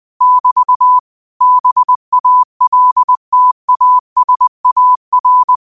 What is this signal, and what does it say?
Morse code, heard as a keyed tone, about 20 wpm, 1000 Hz. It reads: <BT> BALTASAR
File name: qotdfiles / 20wpm